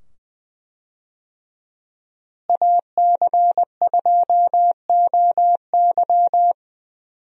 Morse code, heard as a keyed tone, about 20 wpm, 700 Hz. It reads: AC2OY
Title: USA Call Signs - 5 Characters - Repeated 2x - 20wpm